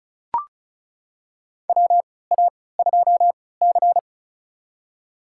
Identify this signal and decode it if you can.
Morse code, heard as a keyed tone, about 36 wpm, spaced as for 20 wpm, 700 Hz. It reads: WA2C
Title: USA Call Signs - 4 Characters - 3x Character Spacing - 35wpm